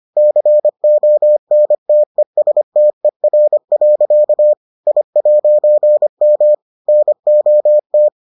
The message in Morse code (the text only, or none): CONTESTER. I'M NOT